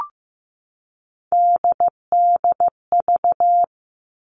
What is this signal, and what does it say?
Morse code, heard as a keyed tone, about 15 wpm, 700 Hz. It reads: DDV
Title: All Cut Numbers - 3-Digit 15wpm